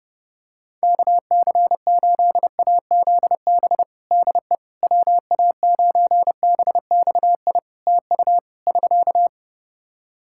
Morse code, heard as a keyed tone, about 30 wpm, 700 Hz. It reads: KC8AZB DE WA9BXI TU <SK>
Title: K1USN SST QSO Practice 30wpm